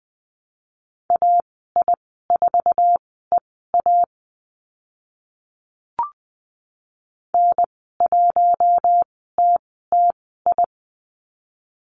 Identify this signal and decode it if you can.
Morse code, heard as a keyed tone, about 20 wpm, spaced as for 14 wpm, 700 Hz. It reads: AI4EA N1TTI
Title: 062 Lesson 11g - Letter h - Call Signs 20wpm